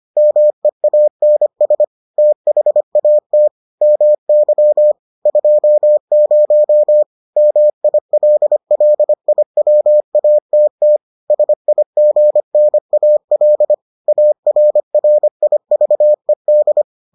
Morse code, heard as a keyed tone, about 25 wpm, 600 Hz. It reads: MEANS THAT MY 20 MILLIWATT SIGNAL ARRIVED